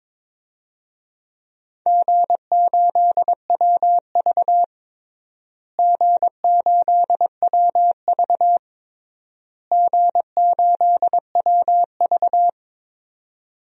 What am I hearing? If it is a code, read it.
Morse code, heard as a keyed tone, about 22 wpm, 700 Hz. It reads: G8WV G8WV G8WV
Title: DXCC-100 Call Signs - 4 Characters - Repeated 3x - 22wpm